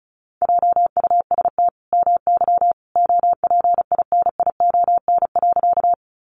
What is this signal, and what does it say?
Morse code, heard as a keyed tone, about 34 wpm, 700 Hz. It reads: JUST MY OPINION.
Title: Sentences from Top 1200 Words 35wpm